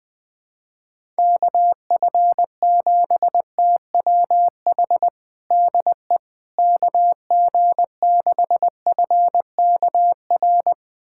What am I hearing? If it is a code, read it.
Morse code, heard as a keyed tone, about 21 wpm, 700 Hz. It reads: KF7TWH DE KG6FKR
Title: USA Callsign DE Callsign - 6 Characters - 20wpm